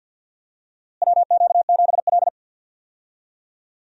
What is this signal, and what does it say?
Morse code, heard as a keyed tone, about 50 wpm, 700 Hz. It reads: WQ7L
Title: USA Call Signs - 4 Characters - Repeated 2x - 50wpm